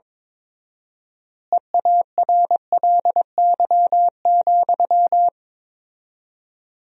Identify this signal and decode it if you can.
Morse code, heard as a keyed tone, about 22 wpm, 700 Hz. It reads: EARLY,
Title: 4x Word-Spacing - Sentences from Top 400 Words - 22wpm